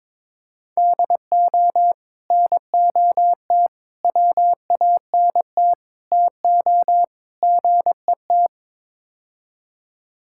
Morse code, heard as a keyed tone, about 22 wpm, 700 Hz. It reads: DO NOT WANT TO GET
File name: practice / 22wpm / Build A Phrase - 5-Word Phrases - 22wpm